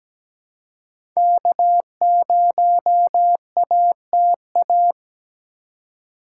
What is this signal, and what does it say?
Morse code, heard as a keyed tone, about 17 wpm, 700 Hz. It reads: K0ATA
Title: USA Call Signs - 5 Characters - 2x Character Spacing - 17wpm